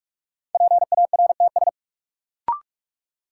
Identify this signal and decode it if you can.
Morse code, heard as a keyed tone, about 46 wpm, 700 Hz. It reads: PARTS
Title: Sets of 3 Words - New Word Drill - 45wpm